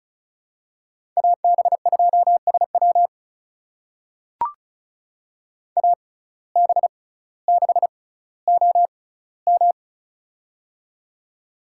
Morse code, heard as a keyed tone, about 34 wpm, spaced as for 18 wpm, 700 Hz. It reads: AB2SW AB6OM
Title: USA Call Signs - 5 Characters - 6x Character Spacing - 35wpm